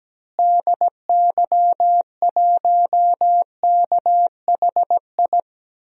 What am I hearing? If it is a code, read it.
Morse code, heard as a keyed tone, about 17 wpm, 700 Hz. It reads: DY1KHI
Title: DXCC-100 Call Signs - 6 Characters - Repeated 2x - 17wpm